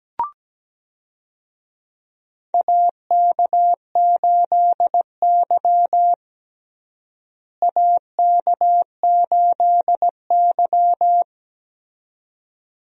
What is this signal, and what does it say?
Morse code, heard as a keyed tone, about 17 wpm, 700 Hz. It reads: AK8Y AK8Y
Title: USA Call Signs - 4 Characters - Repeated 2x - 17wpm